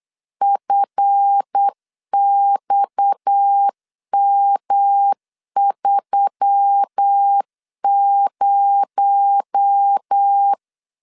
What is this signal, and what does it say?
Morse code, heard as a keyed tone, about 8 wpm, 800 Hz. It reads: FXM30
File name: CW / DET / 1 9 1500C S C 8wpm 5653